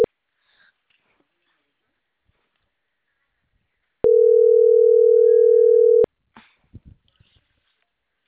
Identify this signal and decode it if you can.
Ringback tone (North America), 440 + 480 Hz.
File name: CENTRAL CAEA / out agcaea01